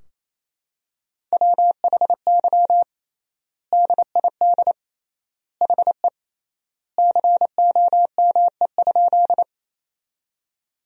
Morse code, heard as a keyed tone, about 28 wpm, 700 Hz. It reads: WHY DID HE COME?
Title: 3x Word-Spacing - Sentences from Top 200 Words - 28wpm